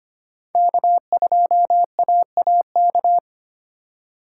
Morse code, heard as a keyed tone, about 25 wpm, 700 Hz. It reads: K2AAK
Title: USA Call Signs - 5 Characters - 6x Character Spacing - 25wpm